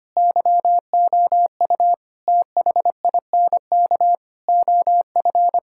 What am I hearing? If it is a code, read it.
Morse code, heard as a keyed tone, about 25 wpm, 700 Hz. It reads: YOU THINK OF
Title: Sentences from Top 1000 Words - Review Part 5 - 25wpm